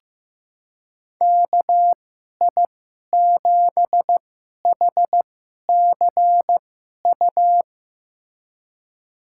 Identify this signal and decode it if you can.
Morse code, heard as a keyed tone, about 15 wpm, spaced as for 11 wpm, 700 Hz. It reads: KI7HCU
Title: USA Call Signs - 6 Characters - 2x Character Spacing - 15wpm